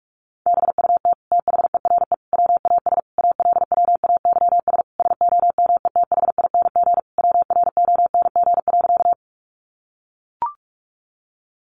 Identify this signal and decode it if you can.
Morse code, heard as a keyed tone, about 44 wpm, 700 Hz. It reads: BUT THERE WAS ALWAYS SOMETHING WRONG.
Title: Sentences from Top 900 Words 45wpm